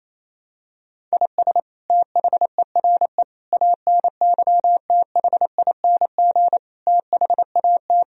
Morse code, heard as a keyed tone, about 28 wpm, 700 Hz. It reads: IS THERE ANYTHING THAT
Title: Sentences from Top 1000 Words - Review Part 1 - 28wpm